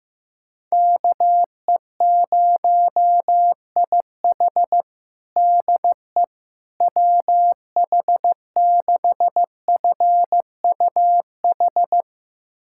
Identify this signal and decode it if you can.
Morse code, heard as a keyed tone, about 16 wpm, 700 Hz. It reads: KE0IH DE WH6FUH
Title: USA Callsign DE Callsign 15wpm